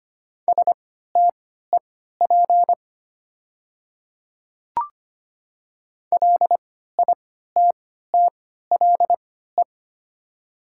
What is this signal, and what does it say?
Morse code, heard as a keyed tone, about 25 wpm, spaced as for 14 wpm, 690 Hz. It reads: STEP LITTLE
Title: 3x Character Spacing - Top 500 Words - 25wpm